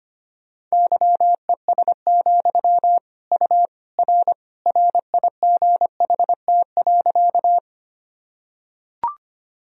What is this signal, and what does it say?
Morse code, heard as a keyed tone, about 25 wpm, 700 Hz. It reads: YES, U R RIGHT.
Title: Sentences from Top 500 Words and Abbreviations 25wpm